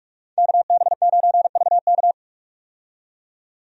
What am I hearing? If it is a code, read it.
Morse code, heard as a keyed tone, about 41 wpm, 700 Hz. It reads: KB9VK
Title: USA Call Signs - 5 Characters - 45wpm